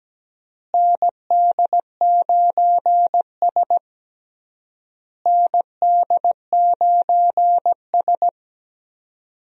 Morse code, heard as a keyed tone, about 17 wpm, 700 Hz. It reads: ND9S ND9S
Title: USA Call Signs - 4 Characters - Repeated 3x - 17wpm